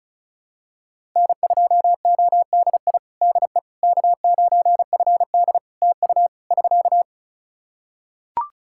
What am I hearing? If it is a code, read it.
Morse code, heard as a keyed tone, about 34 wpm, 700 Hz. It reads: N2ODI DE K9FD TU <SK>
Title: Practice QSO - Exchange 4 - Ending - 35wpm